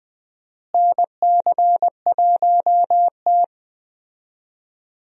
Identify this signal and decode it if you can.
Morse code, heard as a keyed tone, about 21 wpm, 700 Hz. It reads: NC1T
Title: USA Call Signs - 4 Characters - Repeated 3x - 20wpm